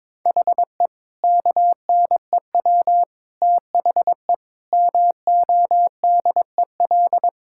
Morse code, heard as a keyed tone, about 22 wpm, 700 Hz. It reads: HE KNEW THE MODEL
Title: Sentences from Top 1000 Words - Review Part 5 - 22wpm